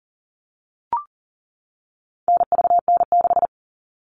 Morse code, heard as a keyed tone, about 40 wpm, 700 Hz. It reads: NVN6
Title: All Cut Numbers - 4-Digit 40wpm